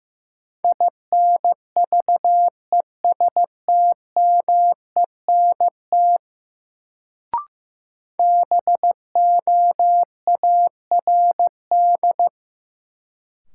Morse code, heard as a keyed tone, about 15 wpm, 700 Hz. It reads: INVESTMENT BOARD
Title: Sentences from Top 700 Words - New Word Drill - 15wpm